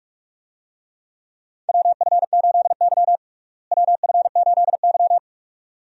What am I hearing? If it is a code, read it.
Morse code, heard as a keyed tone, about 46 wpm, 700 Hz. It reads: WF8Y WF8Y
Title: USA Call Signs - 4 Characters - Repeated 2x - 45wpm